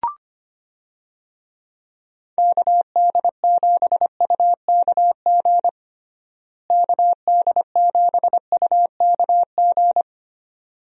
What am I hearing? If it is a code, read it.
Morse code, heard as a keyed tone, about 25 wpm, 700 Hz. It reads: KD7UKG KD7UKG
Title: USA Call Signs - 6 Characters - Repeated 2x - 25wpm